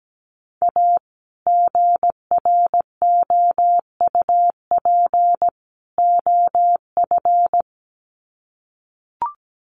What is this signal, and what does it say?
Morse code, heard as a keyed tone, about 17 wpm, 700 Hz. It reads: A GROUP OF
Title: Sets of 3 Words 17wpm